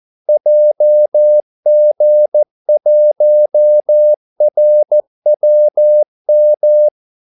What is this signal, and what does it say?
Morse code, heard as a keyed tone, about 14 wpm, 600 Hz. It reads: JG1RWM
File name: Callsigns-14wpm